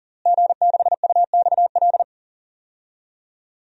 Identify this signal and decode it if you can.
Morse code, heard as a keyed tone, about 40 wpm, 700 Hz. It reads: G6UXL
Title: DXCC-100 Call Signs - 5 Characters - Repeated 3x - 40wpm